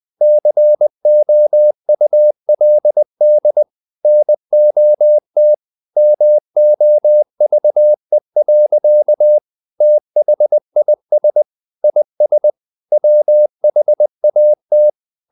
Morse code, heard as a keyed tone, about 20 wpm, 600 Hz. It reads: COULD NOT MOVE. THIS IS WHAT
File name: the old man and the sea part 2 20wpm